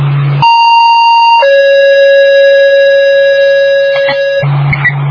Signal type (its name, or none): two-tone page